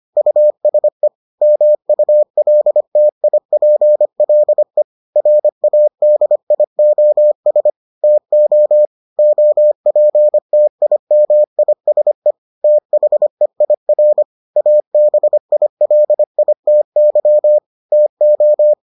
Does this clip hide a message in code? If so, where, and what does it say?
Yes: Morse code from 0.2 to 18.7 s, heard as a keyed tone, about 25 wpm, 600 Hz. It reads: USE MULTIPLE RADIOS TO OPTIMISE THEIR ABILITY TO